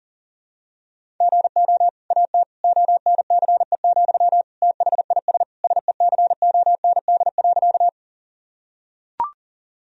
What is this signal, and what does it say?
Morse code, heard as a keyed tone, about 37 wpm, 700 Hz. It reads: GO AT ONCE, THIS SECOND.
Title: Sentences from Top 1000 Words - Review Part 1 - 40wpm